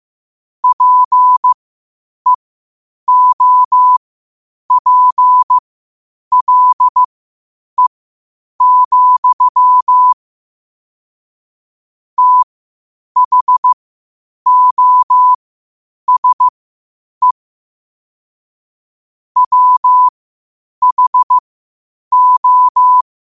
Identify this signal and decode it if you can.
Morse code, heard as a keyed tone, about 15 wpm, spaced as for 8 wpm, 1000 Hz. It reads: PEOPLE, THOSE WHO
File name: qotdfiles / 8wpm